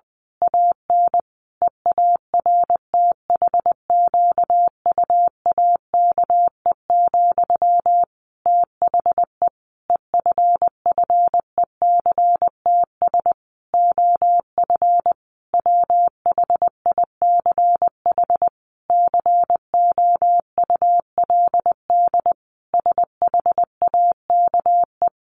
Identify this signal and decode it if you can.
Morse code, heard as a keyed tone, about 20 wpm, 700 Hz. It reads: AN EARTHQUAKE, THE EFFECTS OF WHICH COULD SHAKE
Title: A Sound Of Thunder 20wpm